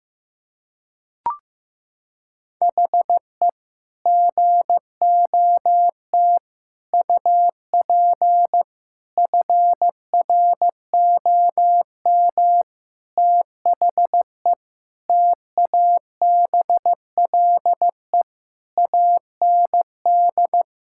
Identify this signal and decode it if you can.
Morse code, heard as a keyed tone, about 15 wpm, 700 Hz. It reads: HE GOT UP FROM THE TABLE AND
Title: Sentences from Top 1600 Words 15wpm